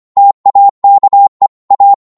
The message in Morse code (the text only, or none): TAKE A